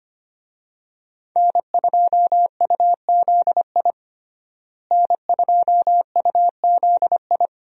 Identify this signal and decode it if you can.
Morse code, heard as a keyed tone, about 25 wpm, 700 Hz. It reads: N2UZI N2UZI
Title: USA Call Signs - 5 Characters - Repeated 2x - 25wpm